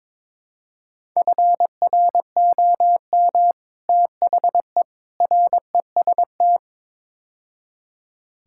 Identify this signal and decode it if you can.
Morse code, heard as a keyed tone, about 22 wpm, 700 Hz. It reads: FROM THE REST
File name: practice / 22wpm / Build A Phrase - 5-Word Phrases - 22wpm